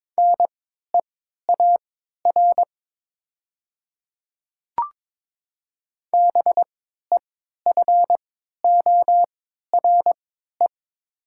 Morse code, heard as a keyed tone, about 22 wpm, spaced as for 12 wpm, 700 Hz. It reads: NEAR BEFORE
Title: 3x Character Spacing - Top 300 Words - 22wpm